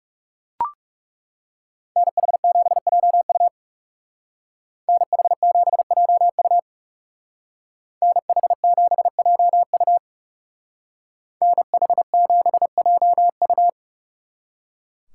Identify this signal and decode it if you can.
Morse code, heard as a keyed tone, about 39 wpm, 700 Hz. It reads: NH7JU NH7JU NH7JU NH7JU